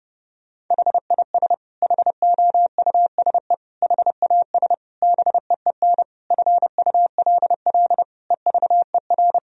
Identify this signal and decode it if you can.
Morse code, heard as a keyed tone, about 32 wpm, 700 Hz. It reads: HIS HOUSE HAS BEEN FULL EVER
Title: Sentences from Top 1000 Words - Review Part 3 - 30wpm